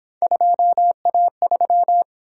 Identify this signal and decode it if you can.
Morse code, heard as a keyed tone, about 29 wpm, 700 Hz. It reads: 2A3